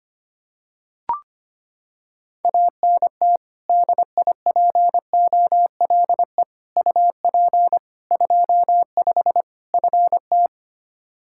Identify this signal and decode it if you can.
Morse code, heard as a keyed tone, about 25 wpm, 700 Hz. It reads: ANT DIPOLE UP 25 FT
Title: Practice QSO - Exchange 2 - Info - 25wpm